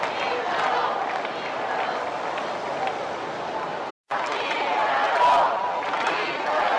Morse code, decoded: none